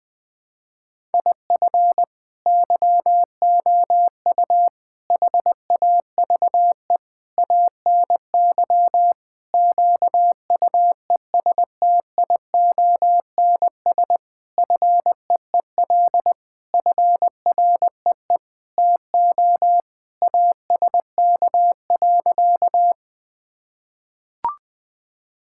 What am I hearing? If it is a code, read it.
Morse code, heard as a keyed tone, about 21 wpm, 700 Hz. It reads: IF YOU HAVE ANY QUESTIONS FEEL FREE TO ASK.
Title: Sentences with 8-Word Phrases - 20wpm